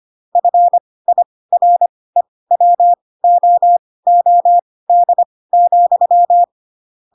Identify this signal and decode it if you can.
Morse code, heard as a keyed tone, about 25 wpm, spaced as for 18 wpm, 700 Hz. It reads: FIREWOOD,
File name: Pinocchio / Pinocchio ch1 12 25 ik1hge